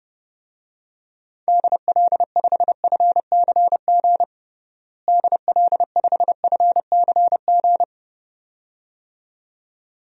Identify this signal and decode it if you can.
Morse code, heard as a keyed tone, about 32 wpm, 700 Hz. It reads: DL5FCG DL5FCG
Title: DXCC-100 Call Signs - 6 Characters - Repeated 2x - 30wpm